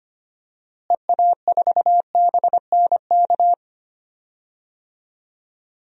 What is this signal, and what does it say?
Morse code, heard as a keyed tone, about 25 wpm, 700 Hz. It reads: EA4BNK
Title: DXCC-100 Call Signs - 6 Characters - Repeated 2x - 25wpm